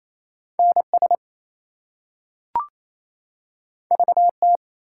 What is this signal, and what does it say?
Morse code, heard as a keyed tone, about 28 wpm, 700 Hz. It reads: NS VT
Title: US States and Canadian Provinces 28wpm